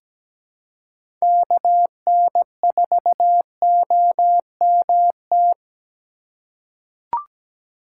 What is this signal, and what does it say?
Morse code, heard as a keyed tone, about 17 wpm, 700 Hz. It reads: KN4OMT
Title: USA Call Signs - 6 Characters - 4x Character Spacing - 17wpm